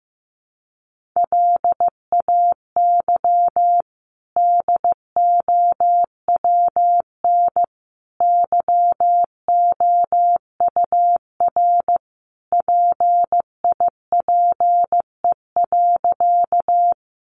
Morse code, heard as a keyed tone, about 15 wpm, 700 Hz. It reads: LAY DOWN YOUR PIPE.
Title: Sentences from Top 1500 Words 15wpm